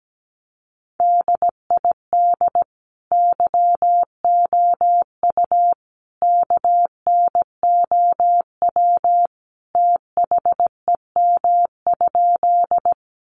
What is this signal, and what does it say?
Morse code, heard as a keyed tone, about 17 wpm, 700 Hz. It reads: DID YOU KNOW THEM?
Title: Sentences from Top 100 Words 17wpm